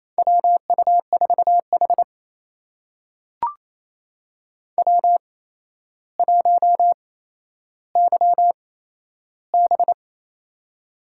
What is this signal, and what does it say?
Morse code, heard as a keyed tone, about 27 wpm, spaced as for 12 wpm, 700 Hz. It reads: WU4H W1YB
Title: USA Call Signs - 4 Characters - 8x Character Spacing - 28wpm